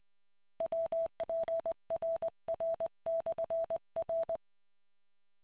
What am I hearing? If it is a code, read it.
Morse code, heard as a keyed tone, about 20 wpm, 660 Hz. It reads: WPRR/R